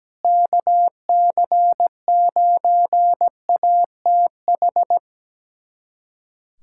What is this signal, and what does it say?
Morse code, heard as a keyed tone, about 17 wpm, 700 Hz. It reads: KC9ATH
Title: USA Callsign DE Callsign - 6 Characters - 17wpm